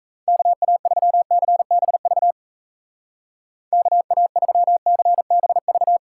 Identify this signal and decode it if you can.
Morse code, heard as a keyed tone, about 35 wpm, 700 Hz. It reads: KA3CBV KA3CBV
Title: Speed Racing - USA 2x3 Call Signs 28wpm